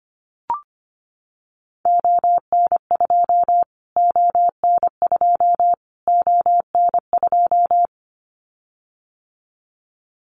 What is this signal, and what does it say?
Morse code, heard as a keyed tone, about 25 wpm, 700 Hz. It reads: ON2 ON2 ON2
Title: Callsign Prefixes for Top 25 DXCC - Repeated 3x - 25wpm